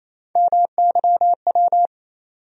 Morse code, heard as a keyed tone, about 28 wpm, 700 Hz. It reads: MYW
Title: USA Call Sign Postfixes - 3 Letters 28wpm